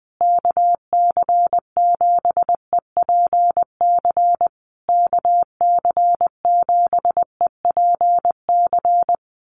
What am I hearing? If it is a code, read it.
Morse code, heard as a keyed tone, about 20 wpm, 700 Hz. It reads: KC7EPC KC7EPC
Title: USA Call Signs Repeated 3x 20wpm